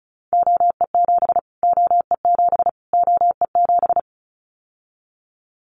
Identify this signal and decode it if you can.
Morse code, heard as a keyed tone, about 34 wpm, 700 Hz. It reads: OE7 OE7 OE7
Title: Callsign Prefixes for Top 25 DXCC - Repeated 3x - 35wpm